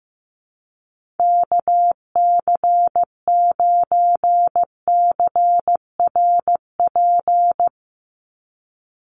Morse code, heard as a keyed tone, about 15 wpm, 700 Hz. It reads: KC9CRP
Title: USA Call Signs 15wpm